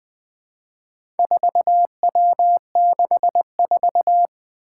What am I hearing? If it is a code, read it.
Morse code, heard as a keyed tone, about 20 wpm, 700 Hz. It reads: 4W64
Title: Four Challenging Letter-Number Combinations 20wpm